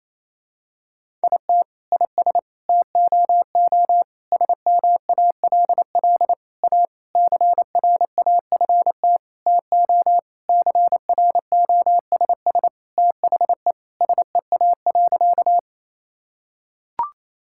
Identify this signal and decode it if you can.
Morse code, heard as a keyed tone, about 28 wpm, 700 Hz. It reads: IT IS TOO SMALL A CRAFT TO CROSS THE SEA.
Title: Sentences from Top 1000 Words - Review Part 6 - 28wpm